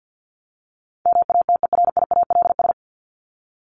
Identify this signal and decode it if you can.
Morse code, heard as a keyed tone, about 50 wpm, 700 Hz. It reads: MATERIALS
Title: ES:S Suffix - Top 100 Words with Base Words 50wpm